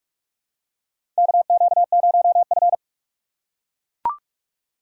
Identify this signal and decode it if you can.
Morse code, heard as a keyed tone, about 44 wpm, 700 Hz. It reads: KQ0F
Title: USA Call Signs - 4 Characters - 3x Character Spacing - 45wpm